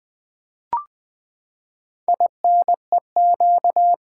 Morse code, heard as a keyed tone, about 20 wpm, 700 Hz. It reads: INEQ
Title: Build A Word - 4 Letters At A Time - 10-Letter Words - 20wpm